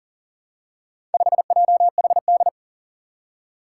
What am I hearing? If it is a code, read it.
Morse code, heard as a keyed tone, about 40 wpm, 700 Hz. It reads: 5JHD
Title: Four Challenging Letter-Number Combinations 40wpm